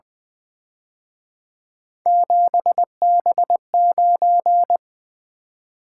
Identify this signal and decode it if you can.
Morse code, heard as a keyed tone, about 24 wpm, 700 Hz. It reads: 7B9
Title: Speed-Racing - Three Letter-Number Combinations - 20wpm